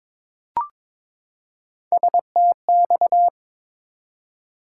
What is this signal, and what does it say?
Morse code, heard as a keyed tone, about 22 wpm, 700 Hz. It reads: STX
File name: ARRL Sections v2 - 22wpm